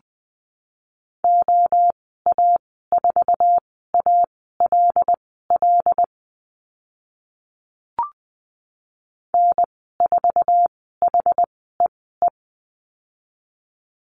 Morse code, heard as a keyed tone, about 20 wpm, spaced as for 15 wpm, 700 Hz. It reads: OA4ALL N4HEE
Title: 076 Lesson 13g - Letter L - Call Signs 20wpm